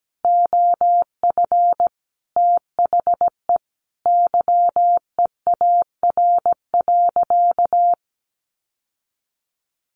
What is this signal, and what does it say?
Morse code, heard as a keyed tone, about 17 wpm, 700 Hz. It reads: OF THE YEAR.
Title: Sentences from Top 1400 Words 17wpm